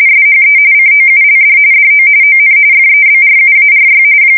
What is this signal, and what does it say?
RTTY, about 45.45 baud. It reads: SHE COULD DO ON FEBRUARY
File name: fsk170 / USS-New-Jersey